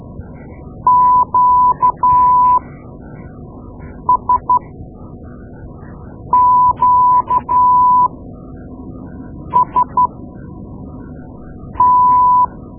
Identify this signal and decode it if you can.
Morse code, heard as a keyed tone, about 13 wpm, spaced as for 5 wpm, 1000 Hz. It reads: QSQST
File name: Morse Lessons / Morse Lessons / Tape-2A